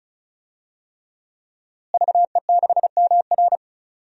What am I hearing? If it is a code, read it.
Morse code, heard as a keyed tone, about 36 wpm, 700 Hz. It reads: VE6MR
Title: DXCC-100 Call Signs - 5 Characters - Repeated 3x - 35wpm